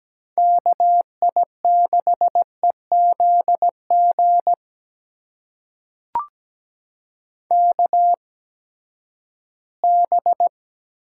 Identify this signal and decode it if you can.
Morse code, heard as a keyed tone, about 17 wpm, spaced as for 12 wpm, 700 Hz. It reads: KI6EZG KB
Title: USA Call Signs - 6 Characters - 8x Character Spacing - 17wpm